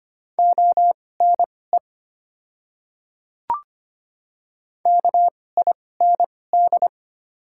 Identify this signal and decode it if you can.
Morse code, heard as a keyed tone, about 25 wpm, spaced as for 18 wpm, 690 Hz. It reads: ONE KIND
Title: 2x Character Spacing - Top 200 Words - 25wpm